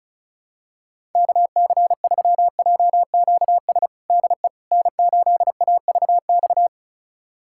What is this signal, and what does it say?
Morse code, heard as a keyed tone, about 34 wpm, 700 Hz. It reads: KC3JQS DE N8AVX
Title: USA Callsign DE Callsign 35wpm